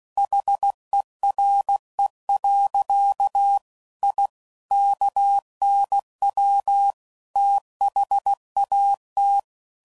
Morse code, heard as a keyed tone, about 16 wpm, 800 Hz. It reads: HERE. I KNW THAT
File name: CW Code Groups @15WPM / QSOs / WD0FBT